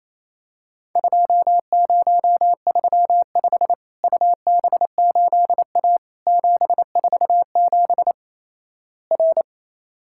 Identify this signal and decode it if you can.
Morse code, heard as a keyed tone, about 28 wpm, 700 Hz. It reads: 2035 UB8A 747 R
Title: Worked All Europe DX Contest - QTC Exchange - 28wpm